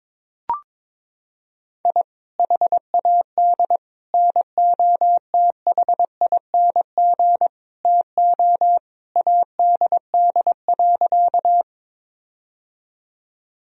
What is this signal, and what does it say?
Morse code, heard as a keyed tone, about 22 wpm, 700 Hz. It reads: I HAD NOTHING TO ADD.
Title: Sentences from Top 1000 Words - Review Part 1 - 22wpm